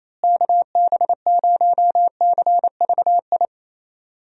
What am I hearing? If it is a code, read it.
Morse code, heard as a keyed tone, about 28 wpm, 700 Hz. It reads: KB0CVI
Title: USA Call Signs - 6 Characters - Repeated 2x - 28wpm